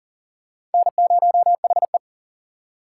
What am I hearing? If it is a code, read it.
Morse code, heard as a keyed tone, about 40 wpm, 700 Hz. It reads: N0HE
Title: USA Call Signs - 4 Characters - 4x Character Spacing - 40wpm